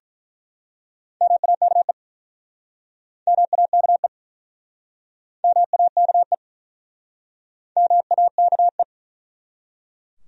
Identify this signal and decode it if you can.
Morse code, heard as a keyed tone, about 47 wpm, 700 Hz. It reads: MAKE MAKE MAKE MAKE